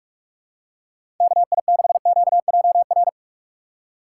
Morse code, heard as a keyed tone, about 44 wpm, 700 Hz. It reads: KI6QJR
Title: USA Call Signs - 6 Characters - 6x Character Spacing - 45wpm